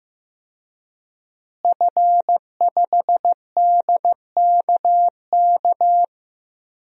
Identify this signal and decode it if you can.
Morse code, heard as a keyed tone, about 15 wpm, 700 Hz. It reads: F5DKK
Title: DXCC-100 Call Signs - 5 Characters - Repeated 3x - 15wpm